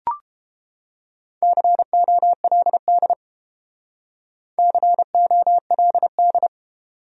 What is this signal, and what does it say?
Morse code, heard as a keyed tone, about 30 wpm, 700 Hz. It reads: COLD COLD